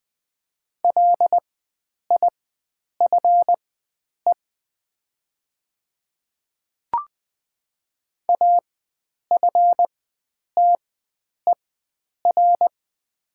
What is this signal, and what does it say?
Morse code, heard as a keyed tone, about 20 wpm, spaced as for 9 wpm, 700 Hz. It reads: LIFE AFTER
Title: 4x Character Spacing - Top 300 Words - 20wpm